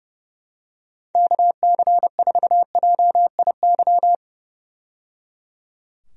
Morse code, heard as a keyed tone, about 30 wpm, 700 Hz. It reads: KC4JIY
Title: USA Call Signs - 6 Characters - 30wpm